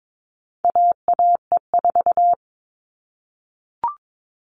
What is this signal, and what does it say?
Morse code, heard as a keyed tone, about 22 wpm, 700 Hz. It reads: AAE4
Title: All Cut Numbers - 4-Digit 22wpm